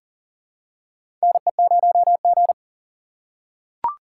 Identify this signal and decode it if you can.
Morse code, heard as a keyed tone, about 40 wpm, 700 Hz. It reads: NE0G
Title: USA Call Signs - 4 Characters - 4x Character Spacing - 40wpm